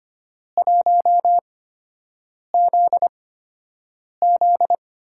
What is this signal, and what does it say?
Morse code, heard as a keyed tone, about 25 wpm, spaced as for 7 wpm, 700 Hz. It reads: 1ZZ
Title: USA Call Signs - 6 Characters - 8x Character Spacing - 25wpm